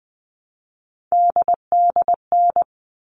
Morse code, heard as a keyed tone, about 20 wpm, 700 Hz. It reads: DDN
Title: All Cut Numbers - 3-Digit 20wpm